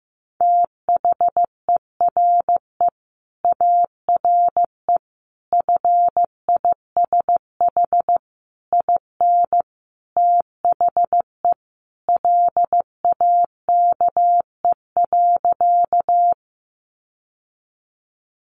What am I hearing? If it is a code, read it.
Morse code, heard as a keyed tone, about 15 wpm, 700 Hz. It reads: THERE ARE FISH IN THE LAKE.
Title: Sentences from Top 1100 Words 15wpm